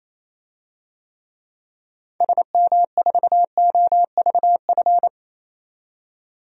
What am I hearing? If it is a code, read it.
Morse code, heard as a keyed tone, about 28 wpm, 700 Hz. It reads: SM4OVF
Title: DXCC-100 Call Signs - 6 Characters - 28wpm